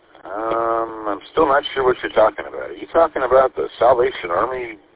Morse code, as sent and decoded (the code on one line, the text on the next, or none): none